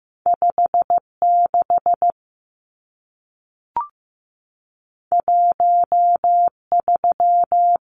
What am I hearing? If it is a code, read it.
Morse code, heard as a keyed tone, about 15 wpm, 700 Hz. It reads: 56 13
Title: Numbers - 2-Digit 15wpm